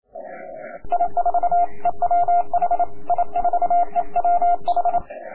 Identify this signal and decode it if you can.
Morse code, heard as a keyed tone, about 28 wpm, 700 Hz. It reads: I4EWH I4EWH
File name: K4M cw 80m